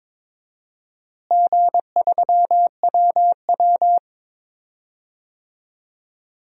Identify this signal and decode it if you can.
Morse code, heard as a keyed tone, about 22 wpm, 700 Hz. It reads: G3WW
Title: DXCC-100 Call Signs - 4 Characters - Repeated 3x - 22wpm